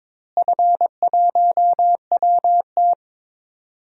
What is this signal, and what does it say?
Morse code, heard as a keyed tone, about 21 wpm, 700 Hz. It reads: F1WT
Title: DXCC-100 Call Signs - 4 Characters - Repeated 3x - 22wpm